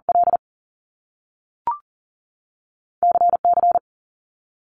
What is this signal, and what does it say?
Morse code, heard as a keyed tone, about 38 wpm, 700 Hz. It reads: L CC
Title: Top Two Letter Combinations in Words 40wpm